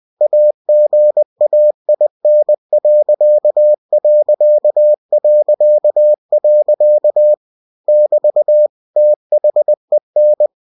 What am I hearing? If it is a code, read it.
Morse code, heard as a keyed tone, about 20 wpm, 600 Hz. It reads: AGAIN.... <BT> THEN
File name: cw-books / winnie-the-pooh / winnie-20wpm-page-0043